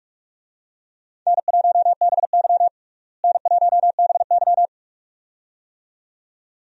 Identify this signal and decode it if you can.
Morse code, heard as a keyed tone, about 45 wpm, 700 Hz. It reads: N1BY N1BY
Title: USA Call Signs - 4 Characters - Repeated 2x - 45wpm